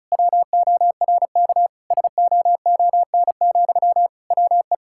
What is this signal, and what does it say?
Morse code, heard as a keyed tone, about 36 wpm, 700 Hz. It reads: WORK SOON, WE
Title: Sentences from Top 1000 Words - Review Part 6 - 35wpm